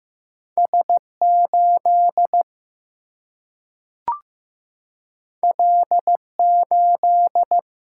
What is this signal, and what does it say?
Morse code, heard as a keyed tone, about 15 wpm, 700 Hz. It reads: S 8 L 8
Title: Single-Letter Single-Number 15wpm